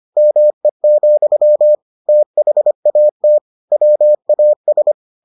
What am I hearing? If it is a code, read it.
Morse code, heard as a keyed tone, about 25 wpm, 600 Hz. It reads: ME, THAT WAS